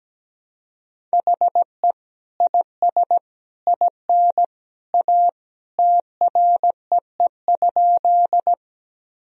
Morse code, heard as a keyed tone, about 17 wpm, 700 Hz. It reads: HE IS IN A TREE?
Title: Sentences from Top 1000 Words - Review Part 1 - 17wpm